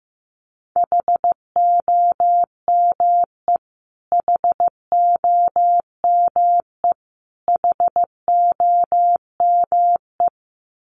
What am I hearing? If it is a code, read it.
Morse code, heard as a keyed tone, about 15 wpm, 700 Hz. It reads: HOME HOME HOME
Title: Top 300 Words Repeated 15wpm